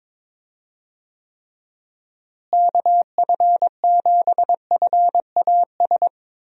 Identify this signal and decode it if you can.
Morse code, heard as a keyed tone, about 22 wpm, 700 Hz. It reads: KF7FAS
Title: USA Call Signs - 6 Characters - Repeated 2x - 22wpm